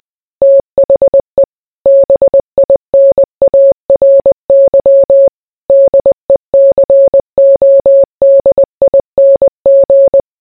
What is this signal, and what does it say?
Morse code, heard as a keyed tone, about 20 wpm, 550 Hz. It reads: THE BINARY DECODING